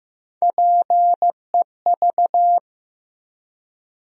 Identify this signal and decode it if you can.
Morse code, heard as a keyed tone, about 15 wpm, 700 Hz. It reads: PEV
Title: USA Call Sign Postfixes - 3 Letters 15wpm